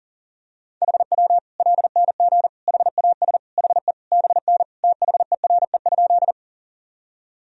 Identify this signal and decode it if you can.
Morse code, heard as a keyed tone, about 40 wpm, 700 Hz. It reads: HW LNG HAS HE BN THERE?
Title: Sentences from Top 100 Words and Abbreviations 40wpm